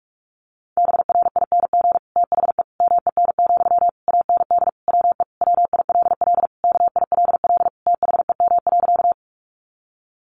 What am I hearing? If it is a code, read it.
Morse code, heard as a keyed tone, about 44 wpm, 700 Hz. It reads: BRING THE MEN, AND WE WILL KILL THEM.
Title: Sentences from Top 1400 Words 45wpm